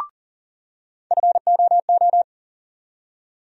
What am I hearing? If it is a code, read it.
Morse code, heard as a keyed tone, about 40 wpm, 700 Hz. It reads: FOO
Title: USA Call Sign Postfixes - 3 Letters 40wpm